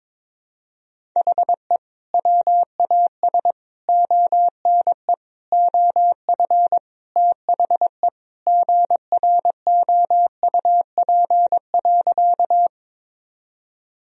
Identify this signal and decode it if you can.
Morse code, heard as a keyed tone, about 21 wpm, 700 Hz. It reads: HE WAS ONE OF THE GROUP.
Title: Sentences from Top 1000 Words - Review Part 1 - 22wpm